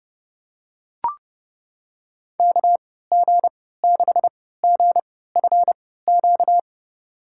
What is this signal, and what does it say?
Morse code, heard as a keyed tone, about 30 wpm, spaced as for 17 wpm, 700 Hz. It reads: KG6GFQ
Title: USA Call Signs - 6 Characters - 3x Character Spacing - 30wpm